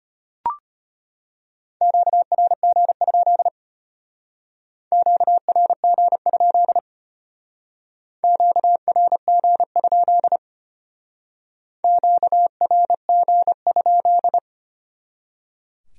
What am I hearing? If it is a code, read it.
Morse code, heard as a keyed tone, about 32 wpm, 700 Hz. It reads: QRG? QRG? QRG? QRG?